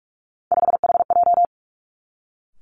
Morse code, heard as a keyed tone, about 46 wpm, 700 Hz. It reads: 5HJ
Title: Three Challenging Letter-Number Combinations 45wpm